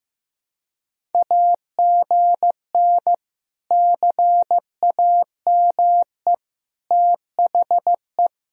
Morse code, heard as a keyed tone, about 15 wpm, 700 Hz. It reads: AGN CAME THE
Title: Sentences from Top 500 Words and Abbreviations 15wpm